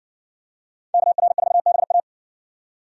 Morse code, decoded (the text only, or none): KR4BA